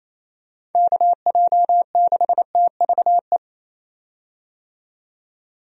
Morse code, heard as a keyed tone, about 28 wpm, 700 Hz. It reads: KJ6TVE
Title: USA Call Signs - 6 Characters - 28wpm